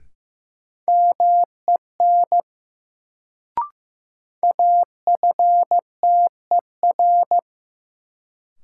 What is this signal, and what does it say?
Morse code, heard as a keyed tone, about 15 wpm, 700 Hz. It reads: MEN AFTER
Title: Sentences from Top 200 Words - New Word Drill - 15wpm